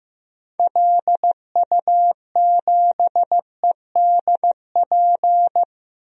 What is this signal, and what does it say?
Morse code, heard as a keyed tone, about 15 wpm, 700 Hz. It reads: LU7EDP
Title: DXCC-100 Call Signs - 6 Characters - Repeated 2x - 15wpm